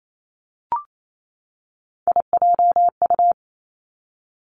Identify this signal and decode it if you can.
Morse code, heard as a keyed tone, about 28 wpm, 700 Hz. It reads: IJU
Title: Three Challenging Letter-Number Combinations 28wpm